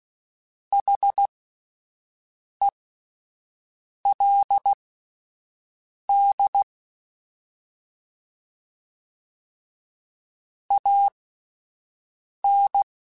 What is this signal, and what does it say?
Morse code, heard as a keyed tone, about 16 wpm, spaced as for 5 wpm, 800 Hz. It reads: HELD AN